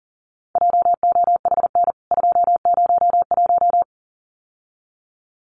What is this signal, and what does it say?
Morse code, heard as a keyed tone, about 40 wpm, 700 Hz. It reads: JOHN 201
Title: CWT Exchange 40wpm